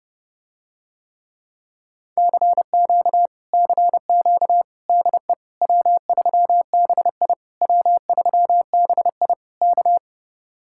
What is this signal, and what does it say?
Morse code, heard as a keyed tone, about 30 wpm, 700 Hz. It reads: CQ CQ DE W3BI W3BI K